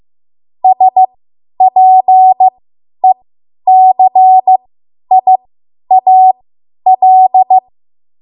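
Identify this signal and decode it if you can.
Morse code, heard as a keyed tone, about 15 wpm, spaced as for 10 wpm, 750 Hz. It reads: SPECIAL
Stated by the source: Morse code practice run at 10 WPM from a code practice archive.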